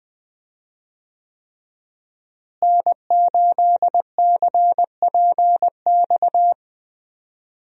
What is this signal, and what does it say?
Morse code, heard as a keyed tone, about 20 wpm, 700 Hz. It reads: N8CPX
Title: USA Call Signs - 5 Characters - Repeated 3x - 20wpm